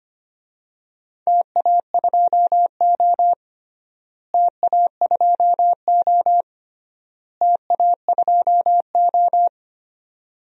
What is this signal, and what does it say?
Morse code, heard as a keyed tone, about 25 wpm, 700 Hz. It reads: TA2O TA2O TA2O
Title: DXCC-100 Call Signs - 4 Characters - Repeated 3x - 25wpm